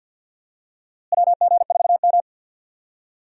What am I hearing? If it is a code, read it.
Morse code, heard as a keyed tone, about 47 wpm, 700 Hz. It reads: WG4M
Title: USA Call Signs - 4 Characters - Repeated 3x - 50wpm